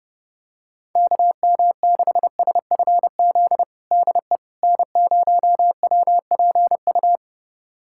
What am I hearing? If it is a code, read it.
Morse code, heard as a keyed tone, about 30 wpm, 700 Hz. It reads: KM6SFZ DE N0WPU
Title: USA Callsign DE Callsign 30wpm